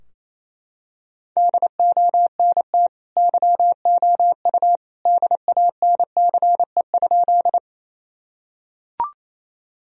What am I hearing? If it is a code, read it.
Morse code, heard as a keyed tone, about 28 wpm, 700 Hz. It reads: DONT YOU DANCE?
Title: Sentences from Top 1000 Words - Review Part 4 - 28wpm